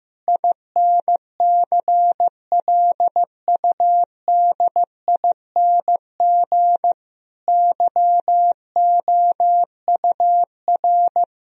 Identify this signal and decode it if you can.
Morse code, heard as a keyed tone, about 15 wpm, 700 Hz. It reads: INCLUDING YOUR
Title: Sentences from Top 1000 Words - Review Part 4 - 15wpm